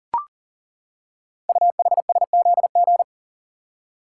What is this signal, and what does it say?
Morse code, heard as a keyed tone, about 42 wpm, 700 Hz. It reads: UHSZG
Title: Code Groups - 5 Letters - 40wpm